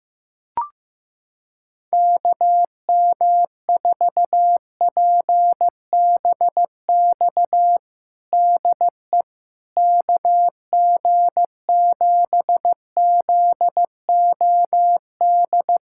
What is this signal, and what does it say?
Morse code, heard as a keyed tone, about 15 wpm, 700 Hz. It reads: KM4PBX DE KG7ZOD
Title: USA Callsign DE Callsign - 6 Characters - 15wpm